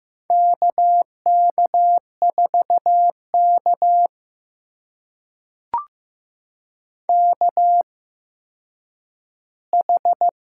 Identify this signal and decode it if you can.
Morse code, heard as a keyed tone, about 15 wpm, spaced as for 9 wpm, 700 Hz. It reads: KK4K KH
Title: USA Call Signs - 4 Characters - 8x Character Spacing - 15wpm